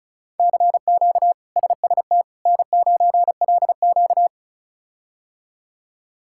Morse code, heard as a keyed tone, about 36 wpm, 700 Hz. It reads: CQ SST N9LQ
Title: K1USN SST QSO Practice - 35wpm